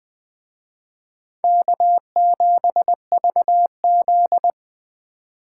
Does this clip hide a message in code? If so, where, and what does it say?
Yes: Morse code from 1.4 to 4.5 s, heard as a keyed tone, about 20 wpm, 700 Hz. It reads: K7VZ